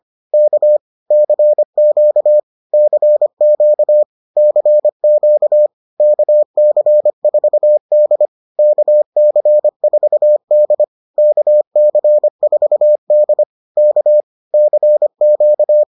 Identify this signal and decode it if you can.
Morse code, heard as a keyed tone, about 25 wpm, 600 Hz. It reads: K CQ CQ CQ KC4D KC4D KC4D K CQ